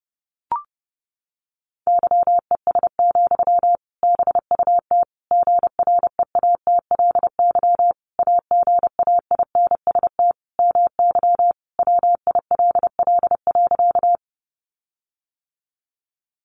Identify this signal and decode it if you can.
Morse code, heard as a keyed tone, about 28 wpm, 700 Hz. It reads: YES, BUT GREATLY AGAINST MY WILL.
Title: Sentences from Top 1400 Words 30wpm